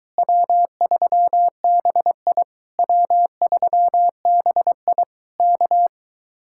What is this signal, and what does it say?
Morse code, heard as a keyed tone, about 23 wpm, 700 Hz. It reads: W3BI W3BI K